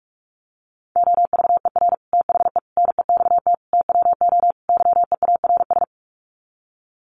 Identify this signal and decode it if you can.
Morse code, heard as a keyed tone, about 44 wpm, 700 Hz. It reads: OVER THE NEXT TWO YEARS
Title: Sets of 5 Words Encore2 45wpm